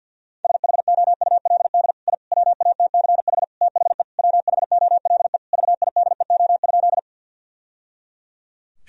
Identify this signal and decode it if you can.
Morse code, heard as a keyed tone, about 50 wpm, 700 Hz. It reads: SHOULD I WATCH THE WHOLE VIDEO?
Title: Sentences from Top 1000 Words - Review Part 4 - 50wpm